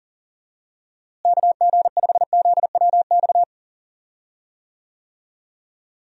Morse code, heard as a keyed tone, about 40 wpm, 700 Hz. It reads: KG5ZWX
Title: USA Call Signs - 6 Characters - Repeated 2x - 40wpm